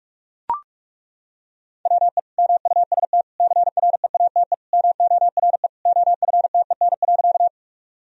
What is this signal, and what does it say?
Morse code, heard as a keyed tone, about 42 wpm, 700 Hz. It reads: WE MUST CREATE MORE OFTEN.
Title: Sentences from Top 1000 Words - Review Part 3 - 45wpm